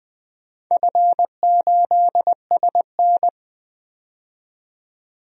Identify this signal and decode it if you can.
Morse code, heard as a keyed tone, about 20 wpm, 700 Hz. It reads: F8SN
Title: DXCC-100 Call Signs - 4 Characters - 20wpm